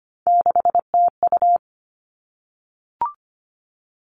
Morse code, heard as a keyed tone, about 25 wpm, 700 Hz. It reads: 6TU
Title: All Cut Numbers - 3-Digit 25wpm